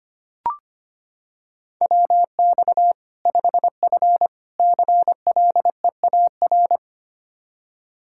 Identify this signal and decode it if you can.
Morse code, heard as a keyed tone, about 25 wpm, 700 Hz. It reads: WX 5F CLEAR
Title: Practice QSO - Exchange 2 - Info - 25wpm